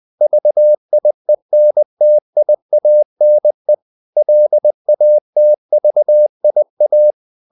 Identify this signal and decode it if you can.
Morse code, heard as a keyed tone, about 20 wpm, 600 Hz. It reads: VIENTIANE LATVIA
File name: CW / Practice / NationalCapitals20